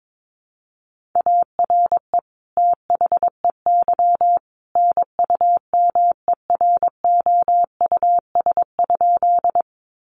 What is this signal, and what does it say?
Morse code, heard as a keyed tone, about 22 wpm, 700 Hz. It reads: ARE THEY NUMEROUS?
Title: Sentences from Top 1400 Words 22wpm